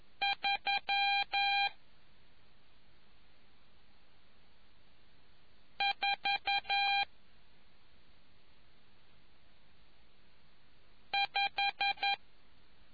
Morse code, heard as a keyed tone, about 11 wpm, 800 Hz. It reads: 3 4 5